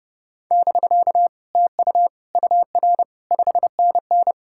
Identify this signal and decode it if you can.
Morse code, heard as a keyed tone, about 40 wpm, 700 Hz. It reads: <BK> TU UR 5NN